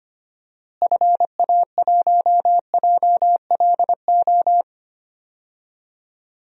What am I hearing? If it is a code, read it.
Morse code, heard as a keyed tone, about 25 wpm, 700 Hz. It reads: FA1JLO
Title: DXCC-100 Call Signs - 6 Characters - Repeated 2x - 25wpm